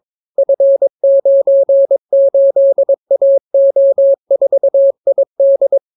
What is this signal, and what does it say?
Morse code, heard as a keyed tone, about 22 wpm, 550 Hz. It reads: F98AO4ID